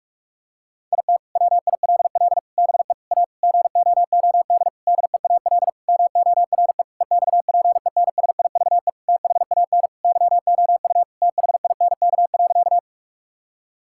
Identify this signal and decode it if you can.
Morse code, heard as a keyed tone, about 41 wpm, 700 Hz. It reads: IT WILL BE A GOOD DEAL MORE EXPENSIVE THAN YOU THINK.
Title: Sentences from Top 1000 Words - Review Part 6 - 45wpm